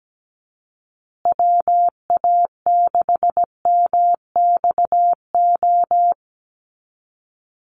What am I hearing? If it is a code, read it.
Morse code, heard as a keyed tone, about 17 wpm, 700 Hz. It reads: WA6MXO
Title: USA Call Signs Repeated 2x 17wpm